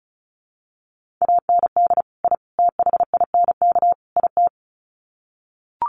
Morse code, heard as a keyed tone, about 34 wpm, 700 Hz. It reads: AND I THINK IT
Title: Sets of 4 Words 35wpm